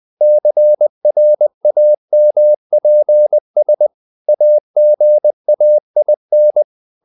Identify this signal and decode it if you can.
Morse code, heard as a keyed tone, about 20 wpm, 600 Hz. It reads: CRAMPS AGAIN
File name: the old man and the sea part 2 20wpm